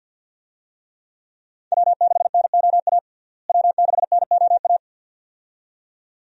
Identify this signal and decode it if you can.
Morse code, heard as a keyed tone, about 47 wpm, 700 Hz. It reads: W6NOA W6NOA
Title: USA Call Signs - 5 Characters - Repeated 2x - 50wpm